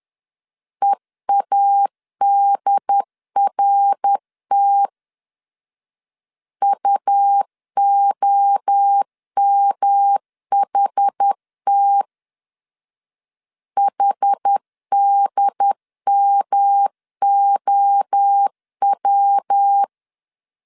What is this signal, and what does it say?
Morse code, heard as a keyed tone, about 10 wpm, 800 Hz. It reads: EADRT UOMHT HDMOW
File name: CW / DET / 1 3 1500C S C 10wpm 3335